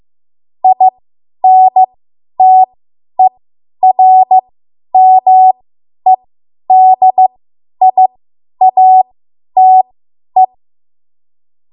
Morse code, heard as a keyed tone, about 15 wpm, spaced as for 10 wpm, 750 Hz. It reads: INTERMEDIATE
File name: Morse / 10WPM